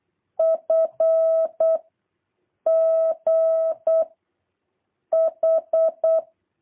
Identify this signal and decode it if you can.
Morse code, heard as a keyed tone, about 8 wpm, 650 Hz. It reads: F G H
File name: jh8cbh / 2023cw